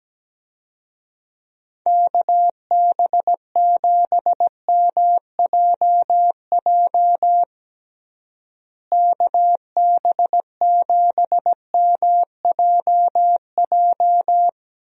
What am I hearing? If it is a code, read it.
Morse code, heard as a keyed tone, about 17 wpm, 700 Hz. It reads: KB7MJJ KB7MJJ
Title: USA Call Signs - 6 Characters - Repeated 3x - 17wpm